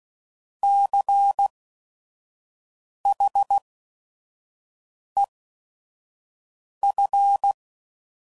Morse code, heard as a keyed tone, about 16 wpm, spaced as for 5 wpm, 800 Hz. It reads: CHEF
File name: Prefixes / By First Letter / Cs